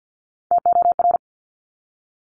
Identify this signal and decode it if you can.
Morse code, heard as a keyed tone, about 50 wpm, 700 Hz. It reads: TOR
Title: Top Three Letter Combinations in Words 50wpm